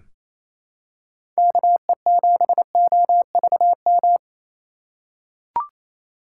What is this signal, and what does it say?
Morse code, heard as a keyed tone, about 28 wpm, 700 Hz. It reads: KE7OVM
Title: USA Call Signs - 6 Characters - 3x Character Spacing - 28wpm